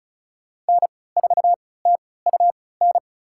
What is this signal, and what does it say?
Morse code, heard as a keyed tone, about 36 wpm, spaced as for 20 wpm, 690 Hz. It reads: N4TUN
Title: USA Call Signs - 5 Characters - 3x Character Spacing - 35wpm